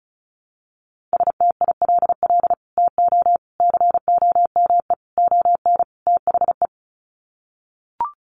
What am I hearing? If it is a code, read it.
Morse code, heard as a keyed tone, about 34 wpm, 700 Hz. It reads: STILL TO COME ON THE
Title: Sets of 5 Words Encore2 35wpm